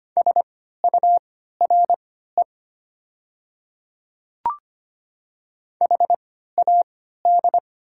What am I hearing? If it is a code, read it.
Morse code, heard as a keyed tone, about 25 wpm, spaced as for 14 wpm, 700 Hz. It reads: SURE HAD
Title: 3x Character Spacing - Top 500 Words - 25wpm